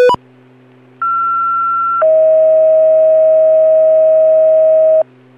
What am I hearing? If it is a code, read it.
Two-tone page, heard as a one-second tone, then a three-second tone, about 1360 then 640 Hz. It is heard from 1.0 to 5.0 s.